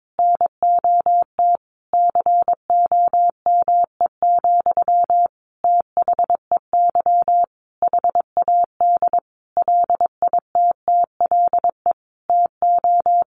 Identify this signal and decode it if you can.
Morse code, heard as a keyed tone, about 22 wpm, 700 Hz. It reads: NOT COME, THEY HAD LITTLE TO
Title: Sentences from Top 1300 Words 22wpm